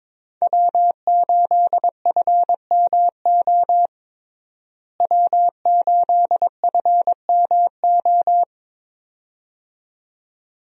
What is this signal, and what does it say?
Morse code, heard as a keyed tone, about 22 wpm, 700 Hz. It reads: W8FMO W8FMO
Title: USA Call Signs - 5 Characters - Repeated 2x - 22wpm